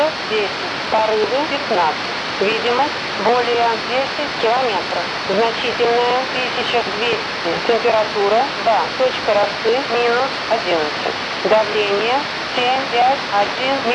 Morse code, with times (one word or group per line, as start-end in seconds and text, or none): none